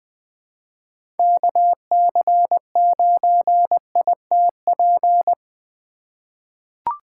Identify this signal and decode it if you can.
Morse code, heard as a keyed tone, about 20 wpm, 700 Hz. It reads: KC9ITP
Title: USA Call Signs - 6 Characters - 2x Character Spacing - 20wpm